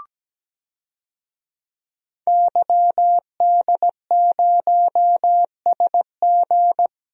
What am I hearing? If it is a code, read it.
Morse code, heard as a keyed tone, about 17 wpm, 700 Hz. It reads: YD0SG
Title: DXCC-100 Call Signs - 5 Characters - 17wpm